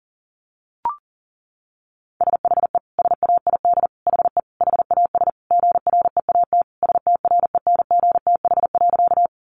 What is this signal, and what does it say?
Morse code, heard as a keyed tone, about 37 wpm, 700 Hz. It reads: SHE SAID HE HAS GREAT STRENGTH.
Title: Sentences from Top 1300 Words 40wpm